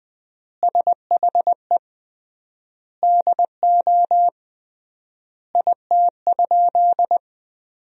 Morse code, heard as a keyed tone, about 20 wpm, 700 Hz. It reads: SHE DO IT?
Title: 3x Word-Spacing - Sentences from Top 100 Words - 20wpm